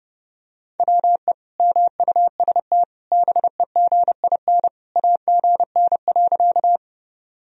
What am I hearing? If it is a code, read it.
Morse code, heard as a keyed tone, about 32 wpm, 700 Hz. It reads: WE MUST BEGIN AGN.
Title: Sentences from Top 300 Words and Abbreviations 30wpm